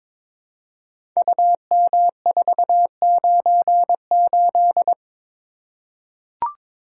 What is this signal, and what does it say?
Morse code, heard as a keyed tone, about 22 wpm, 700 Hz. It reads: UM498
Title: Code Groups - 5 Letter:Numbers - 22wpm